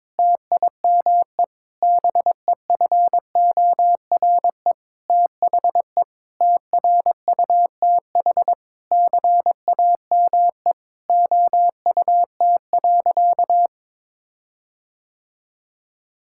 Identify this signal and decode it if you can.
Morse code, heard as a keyed tone, about 22 wpm, 700 Hz. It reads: TIME BEFORE THE TRUTH CAME OUT.
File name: Sentences with 9-Word Phrases - 22wpm